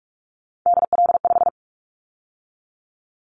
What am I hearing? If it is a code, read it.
Morse code, heard as a keyed tone, about 46 wpm, 700 Hz. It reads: DL5
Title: Callsign Prefixes for Top 25 DXCC - 45wpm